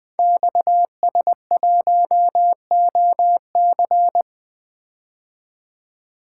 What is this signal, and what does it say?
Morse code, heard as a keyed tone, about 20 wpm, 700 Hz. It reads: XS1OC
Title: Code Groups - 5 Letter:Numbers - 20wpm